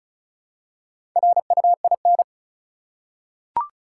Morse code, heard as a keyed tone, about 36 wpm, 700 Hz. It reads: RUIN
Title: Four Letter Words - 35wpm